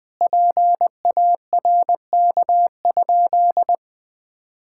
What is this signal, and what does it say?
Morse code, heard as a keyed tone, about 20 wpm, 700 Hz. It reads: PARK?
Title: Sentences with 4-Word Phrases - Spoken Beginning - 20wpm